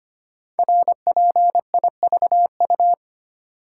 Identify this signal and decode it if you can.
Morse code, heard as a keyed tone, about 25 wpm, 700 Hz. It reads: RPIVU
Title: Code Groups - 5 Letters - 25wpm